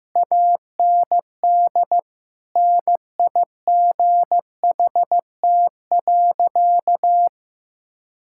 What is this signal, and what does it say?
Morse code, heard as a keyed tone, about 15 wpm, 700 Hz. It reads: AND NIGHT.
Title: Sentences from Top 1000 Words - Review Part 6 - 15wpm